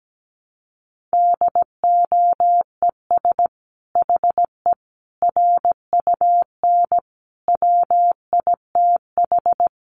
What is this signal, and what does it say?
Morse code, heard as a keyed tone, about 17 wpm, 700 Hz. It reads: DOES HE RUN WITH
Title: Sentences from Top 1300 Words 17wpm